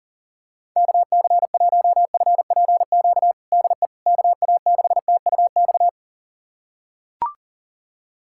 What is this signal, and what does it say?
Morse code, heard as a keyed tone, about 40 wpm, 700 Hz. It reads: KC1FPQ DE KA6TUX
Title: USA Callsign DE Callsign 40wpm